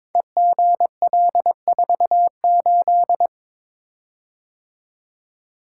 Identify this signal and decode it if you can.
Morse code, heard as a keyed tone, about 22 wpm, 690 Hz. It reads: EGL48
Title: Code Groups - 5 Letter:Numbers - 22wpm